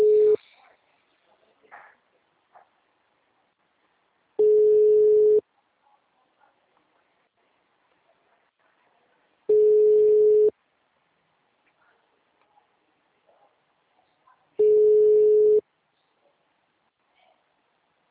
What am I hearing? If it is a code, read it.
Ringback tone (Europe, 425 Hz).
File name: out- -2008- - - .21783